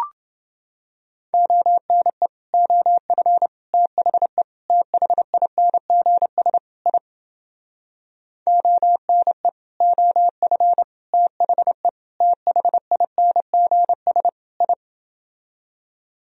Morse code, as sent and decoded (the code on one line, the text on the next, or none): --- -. / --- ..-. / - .... / - .... .. -. --. ... / .. / --- -. / --- ..-. / - .... / - .... .. -. --. ... / ..
ONE OF THE THINGS I ONE OF THE THINGS I